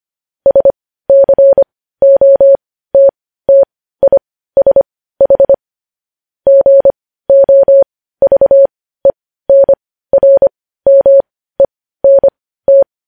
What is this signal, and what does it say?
Morse code, heard as a keyed tone, about 25 wpm, spaced as for 15 wpm, 550 Hz. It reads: SCOTTISH GOVENRMENT
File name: podcast / News headlines-1x-25x15